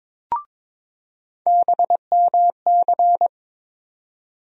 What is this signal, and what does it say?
Morse code, heard as a keyed tone, about 22 wpm, 700 Hz. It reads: BMC
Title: USA Call Sign Postfixes - 3 Letters 22wpm